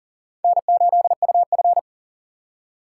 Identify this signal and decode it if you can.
Morse code, heard as a keyed tone, about 40 wpm, 700 Hz. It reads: N8UF
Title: USA Call Signs - 4 Characters - 6x Character Spacing - 40wpm